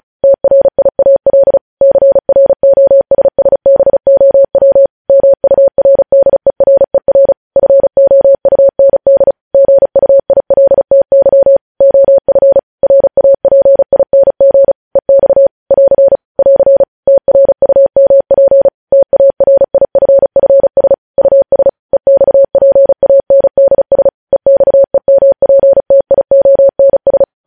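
Morse code, heard as a keyed tone, about 36 wpm, 550 Hz. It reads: TRIAL CROSSBOW MURDERER FOUND GUILTY OF RAPING EX <AR> <AR> TRUMP TARIFFS US EXPANDS EXEMPTIONS
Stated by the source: News headlines sent in Morse code at 35 WPM.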